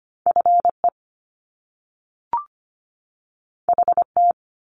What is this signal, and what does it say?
Morse code, heard as a keyed tone, about 25 wpm, 700 Hz. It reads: F E H T
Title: Top Two Letter Combinations in Words 25wpm